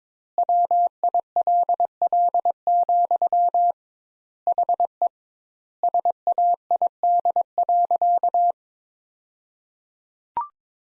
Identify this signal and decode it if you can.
Morse code, heard as a keyed tone, about 22 wpm, 700 Hz. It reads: WILL, HE SAID.
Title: Sentences Top 200 Words - 22wpm